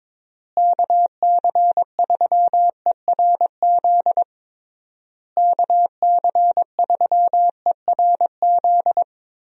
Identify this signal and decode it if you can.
Morse code, heard as a keyed tone, about 22 wpm, 700 Hz. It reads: KC3ERZ KC3ERZ
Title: USA Call Signs - 6 Characters - Repeated 3x - 22wpm